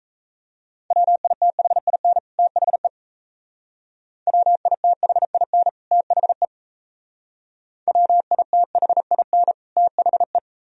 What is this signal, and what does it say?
Morse code, heard as a keyed tone, about 33 wpm, 700 Hz. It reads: WITHIN THE WITHIN THE WITHIN THE
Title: Speed-Racing - Sets of 2 Words 28wpm